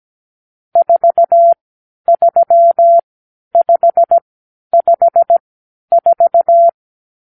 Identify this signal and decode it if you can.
Morse code, heard as a keyed tone, about 17 wpm, spaced as for 10 wpm, 680 Hz. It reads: 43554
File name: k7qo / Lesson 50 (1-6)